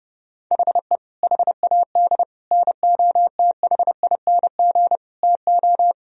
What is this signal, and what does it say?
Morse code, heard as a keyed tone, about 32 wpm, 700 Hz. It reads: HE HAD NOTHING TO
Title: Sentences from Top 1000 Words - Review Part 3 - 30wpm